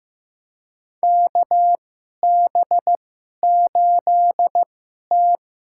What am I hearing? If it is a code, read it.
Morse code, heard as a keyed tone, about 15 wpm, spaced as for 11 wpm, 700 Hz. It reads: KB8T
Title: USA Call Signs - 2x Character Spacing - 15wpm